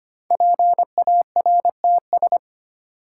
Morse code, heard as a keyed tone, about 25 wpm, 700 Hz. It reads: PARTS
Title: Sets of 3 Words - New Word Drill - 25wpm